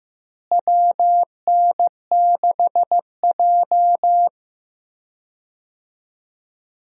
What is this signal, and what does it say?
Morse code, heard as a keyed tone, about 16 wpm, 700 Hz. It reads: WN6J
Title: USA Call Signs - 4 Characters - Repeated 2x - 15wpm